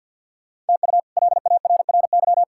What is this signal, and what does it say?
Morse code, heard as a keyed tone, about 48 wpm, 700 Hz. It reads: TU LARRY
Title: K1USN SST QSO Practice - 50wpm